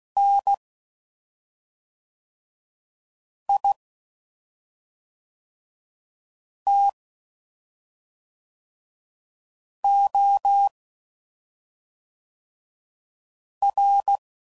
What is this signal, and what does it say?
Morse code, heard as a keyed tone, about 16 wpm, 800 Hz. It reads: N I T O R
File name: Suffixes / Special ING's / SlowToFaster ING's